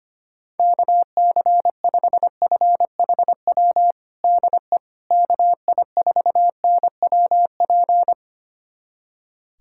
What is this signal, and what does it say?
Morse code, heard as a keyed tone, about 25 wpm, 700 Hz. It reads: KC5FHW DE KI4NWP
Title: USA Callsign DE Callsign 25wpm